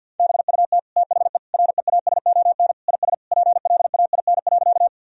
Morse code, heard as a keyed tone, about 51 wpm, 700 Hz. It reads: BUT THE REASON IS PLAIN.
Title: Sentences from Top 1000 Words - Review Part 2 - 50wpm